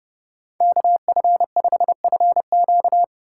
Morse code, heard as a keyed tone, about 27 wpm, 700 Hz. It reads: KF5FQ